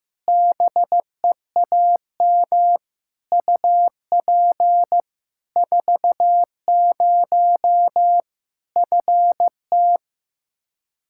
Morse code, heard as a keyed tone, about 15 wpm, 700 Hz. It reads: BEAM UP 40 FT
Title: Practice QSO - Exchange 2 - Info - 15wpm